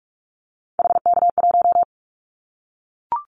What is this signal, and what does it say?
Morse code, heard as a keyed tone, about 44 wpm, 700 Hz. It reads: HK1
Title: Callsign Prefixes for Top 100 DXCC - 45wpm